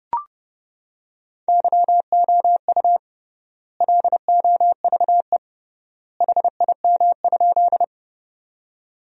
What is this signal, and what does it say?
Morse code, heard as a keyed tone, about 30 wpm, 700 Hz. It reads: YOU LOVE HIM?
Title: 3x Word-Spacing - Sentences from Top 400 Words - 30wpm